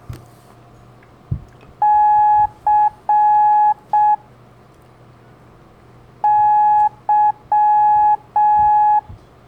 Morse code, heard as a keyed tone, about 6 wpm, 840 Hz. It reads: C Y
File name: blog / LETRAS COM SOM SEMELHANTE MORSE